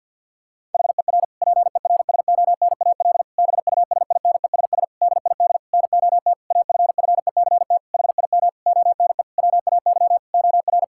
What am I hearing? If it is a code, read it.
Morse code, heard as a keyed tone, about 48 wpm, 700 Hz. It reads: HER PERSONAL BUSINESS DID NOT AFFECT HIM ONE WAY OR
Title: Sentences from Top 1000 Words - Review Part 6 - 50wpm